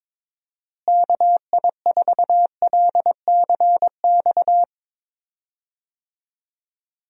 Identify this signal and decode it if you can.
Morse code, heard as a keyed tone, about 22 wpm, 700 Hz. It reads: KI4LCX
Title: USA Call Signs - 6 Characters - Repeated 2x - 22wpm